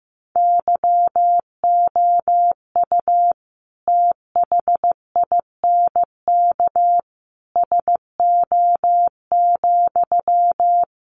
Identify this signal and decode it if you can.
Morse code, heard as a keyed tone, about 14 wpm, 700 Hz. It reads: YOU THINK SO,
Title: Sentences from Top 200 Words 15wpm